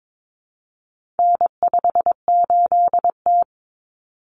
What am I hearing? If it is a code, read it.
Morse code, heard as a keyed tone, about 22 wpm, 700 Hz. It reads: N58T
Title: Common Cut Numbers - 4-Digit 22wpm